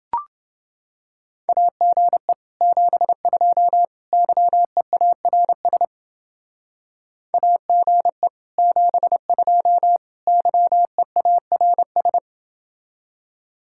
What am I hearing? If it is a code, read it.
Morse code, heard as a keyed tone, about 29 wpm, 700 Hz. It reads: AGE 72 YEARS AGE 72 YEARS